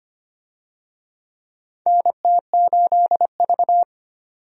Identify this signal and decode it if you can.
Morse code, heard as a keyed tone, about 25 wpm, 700 Hz. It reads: NT8V
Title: USA Call Signs - 4 Characters - 25wpm